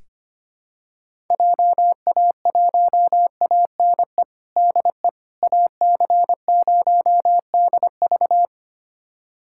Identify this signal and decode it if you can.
Morse code, heard as a keyed tone, about 25 wpm, 700 Hz. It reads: JA1ANE DE AC0DV
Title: USA Callsign DE Callsign 25wpm